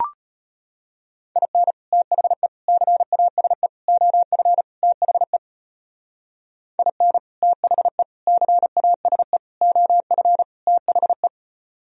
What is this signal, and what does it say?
Morse code, heard as a keyed tone, about 33 wpm, 700 Hz. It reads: IN THE CASE OF THE IN THE CASE OF THE